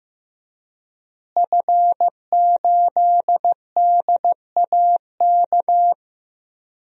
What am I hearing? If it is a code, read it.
Morse code, heard as a keyed tone, about 15 wpm, 700 Hz. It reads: F8DAK
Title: DXCC-100 Call Signs - 5 Characters - Repeated 3x - 15wpm